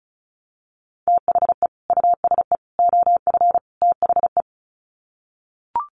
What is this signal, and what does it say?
Morse code, heard as a keyed tone, about 34 wpm, 700 Hz. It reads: THE USE OF THE
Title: Sets of 4 Words 35wpm